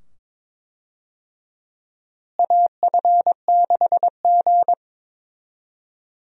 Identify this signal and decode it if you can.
Morse code, heard as a keyed tone, about 22 wpm, 700 Hz. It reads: AF6G
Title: USA Call Signs - 4 Characters - Repeated 2x - 22wpm